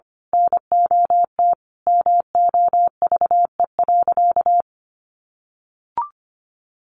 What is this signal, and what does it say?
Morse code, heard as a keyed tone, about 25 wpm, 700 Hz. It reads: NOT MOVE.
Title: A Sound Of Thunder 25wpm